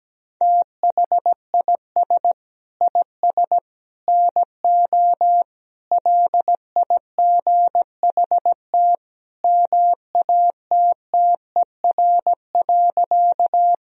Morse code, heard as a keyed tone, about 17 wpm, 700 Hz. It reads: THIS IS NO LIGHT MATTER.
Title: Sentences from Top 1000 Words - Review Part 1 - 17wpm